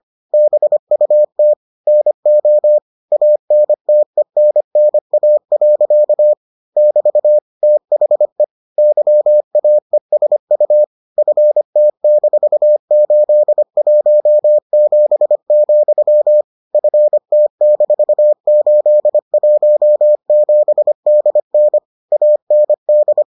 Morse code, heard as a keyed tone, about 25 wpm, 600 Hz. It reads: BUT NO ANTENNA. <BT> THE YAESU FT-817, FT-817DN AND